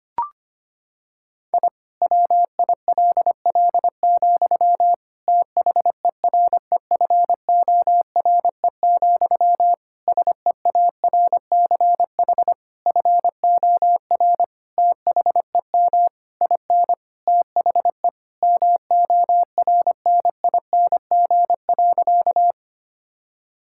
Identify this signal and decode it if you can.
Morse code, heard as a keyed tone, about 25 wpm, 700 Hz. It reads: I WILL, THEREFORE, SEARCH FOR THEM IN THE MORNING.
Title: Sentences from Top 1000 Words - Review Part 5 - 25wpm